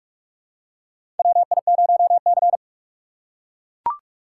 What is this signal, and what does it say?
Morse code, heard as a keyed tone, about 42 wpm, 700 Hz. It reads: WI0C
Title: USA Call Signs - 4 Characters - 6x Character Spacing - 45wpm